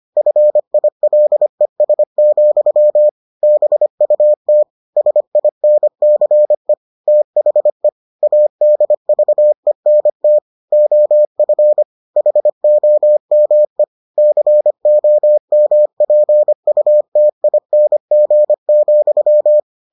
Morse code, heard as a keyed tone, about 25 wpm, 600 Hz. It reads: FILES, BUT SINCE THE ADVENT OF HOME COMPUTING,